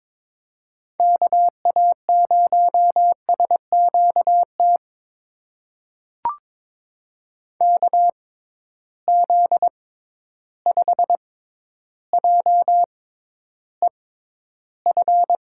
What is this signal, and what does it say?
Morse code, heard as a keyed tone, about 22 wpm, spaced as for 11 wpm, 700 Hz. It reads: KA0SQT KZ5JEF
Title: USA Call Signs - 6 Characters - 6x Character Spacing - 22wpm